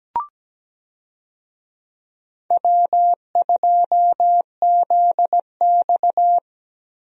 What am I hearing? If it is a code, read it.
Morse code, heard as a keyed tone, about 17 wpm, 700 Hz. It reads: W2ZX
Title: USA Call Signs - 4 Characters - 17wpm